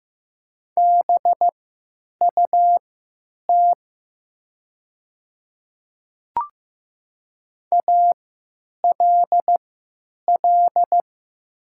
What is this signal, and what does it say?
Morse code, heard as a keyed tone, about 15 wpm, spaced as for 9 wpm, 700 Hz. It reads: BUT ALL
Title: 3x Character Spacing - Top 200 Words - 15wpm